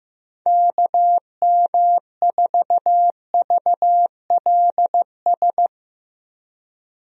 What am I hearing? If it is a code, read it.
Morse code, heard as a keyed tone, about 15 wpm, 700 Hz. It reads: KM4VLS
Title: USA Call Signs - 6 Characters - Repeated 3x - 15wpm